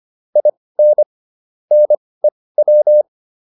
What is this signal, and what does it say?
Morse code, heard as a keyed tone, about 25 wpm, spaced as for 18 wpm, 600 Hz. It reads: IN NEW